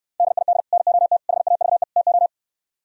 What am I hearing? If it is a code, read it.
Morse code, heard as a keyed tone, about 69 wpm, 700 Hz. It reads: DID NOT HAVE TO